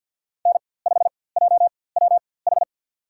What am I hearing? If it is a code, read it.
Morse code, heard as a keyed tone, about 52 wpm, spaced as for 23 wpm, 700 Hz. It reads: N5JWH
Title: USA Call Signs - 5 Characters - 4x Character Spacing - 50wpm